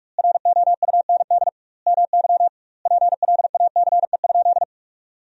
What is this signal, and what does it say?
Morse code, heard as a keyed tone, about 46 wpm, 700 Hz. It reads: ROUND MY PLACE?
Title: 2x Word-Spacing - Sentences from Top 200 Words - 45wpm